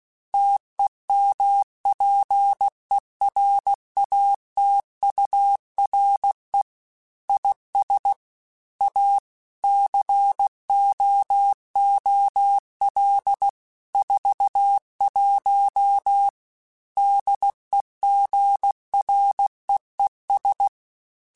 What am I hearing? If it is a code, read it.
Morse code, heard as a keyed tone, about 16 wpm, 800 Hz. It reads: TEMPERATURE IS A COOL 41 DEGREES
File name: Weather Sets / Weather Sets-mp3s / Weather 4 14 @15